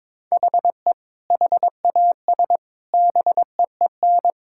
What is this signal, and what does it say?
Morse code, heard as a keyed tone, about 22 wpm, 700 Hz. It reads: HE HAS BEEN
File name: practice / 22wpm / Sentences with 3-Word Phrases - 22wpm